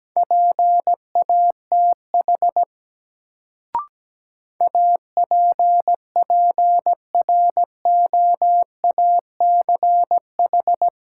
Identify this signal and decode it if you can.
Morse code, heard as a keyed tone, about 17 wpm, 700 Hz. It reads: PATH APPROACH
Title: Sentences from Top 1400 Words - New Word Drill Alt - 17wpm